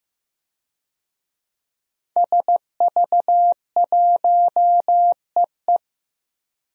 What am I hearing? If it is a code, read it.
Morse code, heard as a keyed tone, about 15 wpm, 700 Hz. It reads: SV1EE
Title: DXCC-100 Call Signs - 5 Characters - Repeated 2x - 15wpm